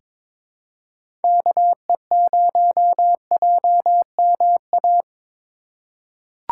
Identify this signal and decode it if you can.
Morse code, heard as a keyed tone, about 23 wpm, 700 Hz. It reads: KE0JMA
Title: USA Call Signs - 6 Characters - 2x Character Spacing - 22wpm